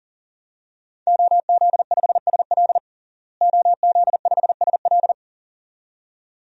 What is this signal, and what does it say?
Morse code, heard as a keyed tone, about 40 wpm, 700 Hz. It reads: OZ5SL OZ5SL
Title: DXCC-100 Call Signs - 5 Characters - Repeated 2x - 40wpm